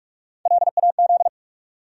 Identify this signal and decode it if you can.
Morse code, heard as a keyed tone, about 46 wpm, 700 Hz. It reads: LAZ
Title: USA Call Sign Postfixes - 3 Letters 45wpm